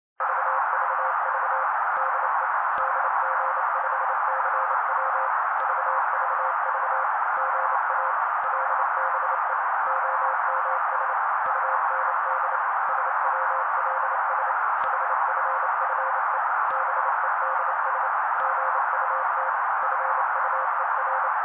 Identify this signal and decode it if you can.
Morse code, heard as a keyed tone, about 27 wpm, 530 Hz. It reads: VVV DE DG5CW VVV GA LBE OMS UND SWLS HFFE BEDS GUT FUER
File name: daily / 27wpm noise